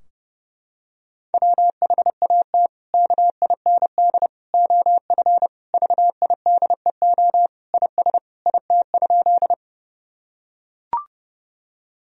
Morse code, heard as a keyed tone, about 32 wpm, 700 Hz. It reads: WHAT KIND OF VIDEO IS IT?
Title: Sentences from Top 1000 Words - Review Part 4 - 30wpm